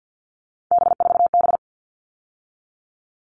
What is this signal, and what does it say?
Morse code, heard as a keyed tone, about 50 wpm, 700 Hz. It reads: B4B
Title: All Cut Numbers - 3-Digit 50wpm